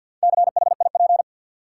Morse code, heard as a keyed tone, about 52 wpm, 700 Hz. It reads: CHIP
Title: Four Letter Words - 50wpm